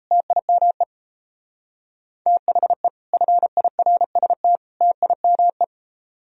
Morse code, heard as a keyed tone, about 37 wpm, 700 Hz. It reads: TIME THE FIRST TIME